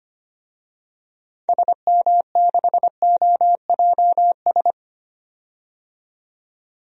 Morse code, heard as a keyed tone, about 25 wpm, 700 Hz. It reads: SM6OJS
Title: DXCC-100 Call Signs - 6 Characters - 25wpm